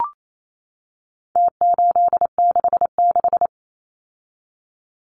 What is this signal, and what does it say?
Morse code, heard as a keyed tone, about 28 wpm, 700 Hz. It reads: T866
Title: Numbers - Contest Style - 4-Digit 28wpm